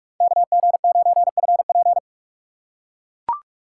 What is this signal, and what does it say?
Morse code, heard as a keyed tone, about 44 wpm, 700 Hz. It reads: KG9FP